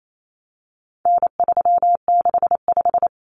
Morse code, heard as a keyed tone, about 28 wpm, 700 Hz. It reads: N365
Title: Common Cut Numbers - 4-Digit 28wpm